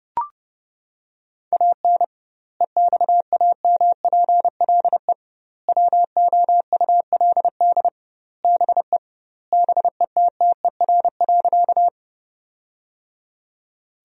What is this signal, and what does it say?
Morse code, heard as a keyed tone, about 32 wpm, 700 Hz. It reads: AN EXAMPLE WOULD BE BETTER.
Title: 2x Word-Spacing - Sentences from Top 400 Words - 30wpm